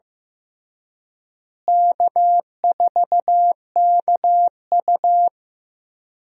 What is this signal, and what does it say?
Morse code, heard as a keyed tone, about 15 wpm, 700 Hz. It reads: K4KU
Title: USA Call Signs - 4 Characters - Repeated 3x - 15wpm